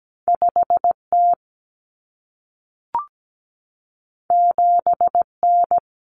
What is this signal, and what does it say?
Morse code, heard as a keyed tone, about 17 wpm, 700 Hz. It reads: 5T 7N
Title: Common Cut Numbers - 2-Digit 17wpm